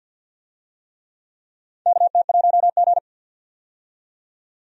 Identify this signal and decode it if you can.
Morse code, heard as a keyed tone, about 51 wpm, 700 Hz. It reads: KT1G
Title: USA Call Signs - 4 Characters - 50wpm